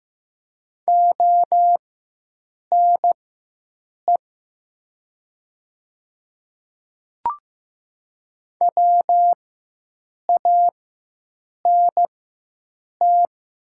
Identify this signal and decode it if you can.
Morse code, heard as a keyed tone, about 15 wpm, spaced as for 7 wpm, 700 Hz. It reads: ONE WANT
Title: 4x Character Spacing - Top 200 Words - 15wpm